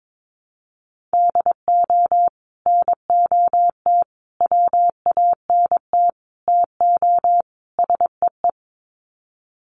Morse code, heard as a keyed tone, about 22 wpm, 700 Hz. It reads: DO NOT WANT TO SEE
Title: Sets of 5 Words 22wpm